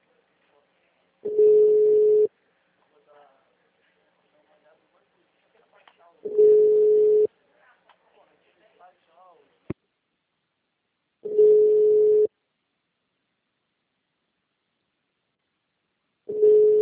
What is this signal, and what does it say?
Ringback tone (Europe, 425 Hz), with speech.
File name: out- -2023- - - .40